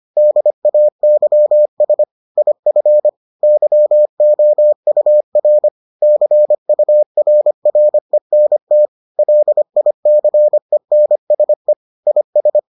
Morse code, heard as a keyed tone, about 25 wpm, 600 Hz. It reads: DAYS IF YOUR CURRENT LICENSE IS